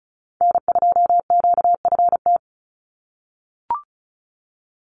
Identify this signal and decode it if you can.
Morse code, heard as a keyed tone, about 34 wpm, 700 Hz. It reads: N2QFT
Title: USA Call Signs 35wpm